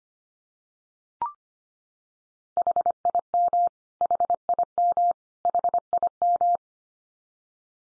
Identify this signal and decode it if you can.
Morse code, heard as a keyed tone, about 25 wpm, 700 Hz. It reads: HIM HIM HIM
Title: Top 100 Words 25wpm x3